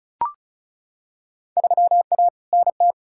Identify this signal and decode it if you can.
Morse code, heard as a keyed tone, about 36 wpm, 700 Hz. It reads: 3A NT
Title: Field Day Exchanges Encore - 35wpm